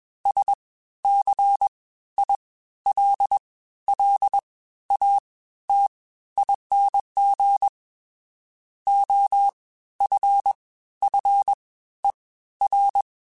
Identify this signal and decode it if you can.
Morse code, heard as a keyed tone, about 21 wpm, spaced as for 13 wpm, 800 Hz. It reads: SCILLATING OFFER